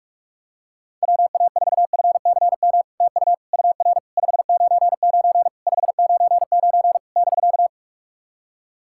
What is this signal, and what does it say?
Morse code, heard as a keyed tone, about 44 wpm, 700 Hz. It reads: WA4FCM TU UR 599 599 <BK>
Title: SOTA QSO Practice 45wpm